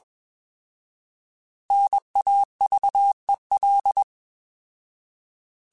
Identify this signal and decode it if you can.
Morse code, heard as a keyed tone, about 21 wpm, 800 Hz. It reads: NAVEL
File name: Alphabetical / Alphabetical mp3s / Ns @20